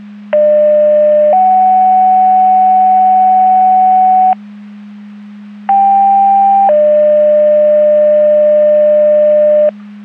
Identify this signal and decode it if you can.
Two-tone page, heard as a one-second tone, then a three-second tone, about 600 then 750 Hz, about 790 then 600 Hz. It is heard from 0.3 to 9.7 s.
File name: Portage 154.385 Paging